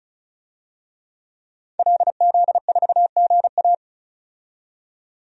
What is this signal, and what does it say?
Morse code, heard as a keyed tone, about 36 wpm, 700 Hz. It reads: LZ4GA
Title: DXCC-100 Call Signs - 5 Characters - 35wpm